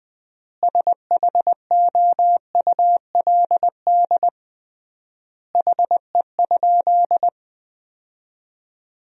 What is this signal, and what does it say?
Morse code, heard as a keyed tone, about 20 wpm, 700 Hz. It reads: SHOULD HE?
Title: 3x Word-Spacing - Sentences from Top 300 Words - 20wpm